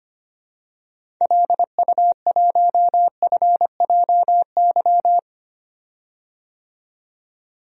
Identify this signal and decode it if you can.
Morse code, heard as a keyed tone, about 25 wpm, 700 Hz. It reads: LU1FJY
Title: DXCC-100 Call Signs - 6 Characters - 25wpm